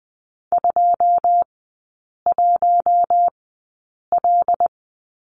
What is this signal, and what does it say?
Morse code, heard as a keyed tone, about 20 wpm, spaced as for 8 wpm, 700 Hz. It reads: 21L
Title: 107 Lesson 18e - Letter m - Three Characters 20wpm